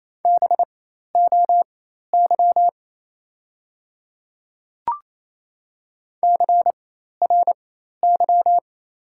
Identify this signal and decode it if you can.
Morse code, heard as a keyed tone, about 28 wpm, spaced as for 13 wpm, 700 Hz. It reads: BOY CRY
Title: 4x Character Spacing - Top 500 Words - 28wpm